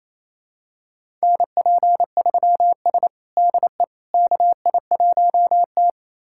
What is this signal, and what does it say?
Morse code, heard as a keyed tone, about 27 wpm, 700 Hz. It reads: NP3S DE KI1T
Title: USA Callsign DE Callsign - 4 Characters - 28wpm